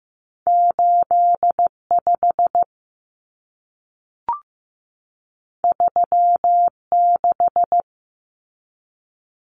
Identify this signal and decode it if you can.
Morse code, heard as a keyed tone, about 15 wpm, 700 Hz. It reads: 85 36
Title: Common Cut Numbers - 2-Digit 15wpm